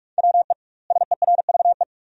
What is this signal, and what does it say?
Morse code, heard as a keyed tone, about 46 wpm, 700 Hz. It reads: WE SERVE
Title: 2x Word-Spacing - Sentences from Top 400 Words - 45wpm